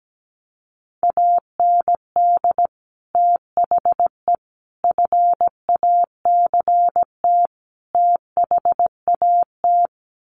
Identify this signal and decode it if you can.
Morse code, heard as a keyed tone, about 17 wpm, 700 Hz. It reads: AND THE FACT THAT
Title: Sets of 4 Words 17wpm